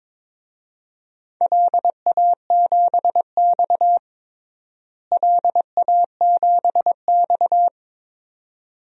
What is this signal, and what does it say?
Morse code, heard as a keyed tone, about 22 wpm, 700 Hz. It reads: LA7X LA7X
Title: DXCC-100 Call Signs - 4 Characters - Repeated 2x - 22wpm